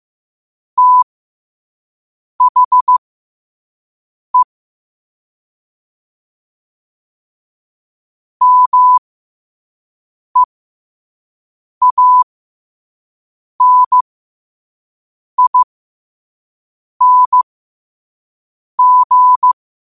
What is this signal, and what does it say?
Morse code, heard as a keyed tone, about 15 wpm, spaced as for 5 wpm, 1000 Hz. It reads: THE MEANING
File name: qotdfiles / 5wpm